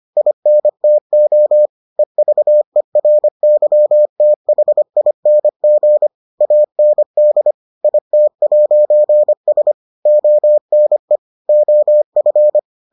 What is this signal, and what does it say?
Morse code, heard as a keyed tone, about 25 wpm, 600 Hz. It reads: INTO EVERYTHING AND IT'S ONE OF